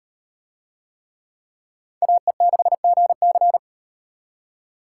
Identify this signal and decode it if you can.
Morse code, heard as a keyed tone, about 38 wpm, 700 Hz. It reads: AE6GC